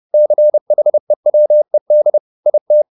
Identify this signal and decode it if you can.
Morse code, heard as a keyed tone, about 30 wpm, 600 Hz. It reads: CHEWED IT
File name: the old man and the sea / the old man and the sea 1 30wpm